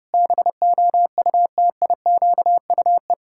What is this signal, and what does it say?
Morse code, heard as a keyed tone, about 30 wpm, 700 Hz. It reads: BOUTIQUE
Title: Pangram Challenge v2 - 30wpm